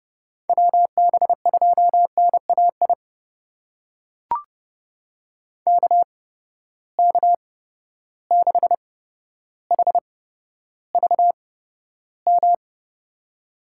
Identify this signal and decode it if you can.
Morse code, heard as a keyed tone, about 31 wpm, spaced as for 13 wpm, 700 Hz. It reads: WB2NAI KK6HVM
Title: USA Call Signs - 6 Characters - 8x Character Spacing - 30wpm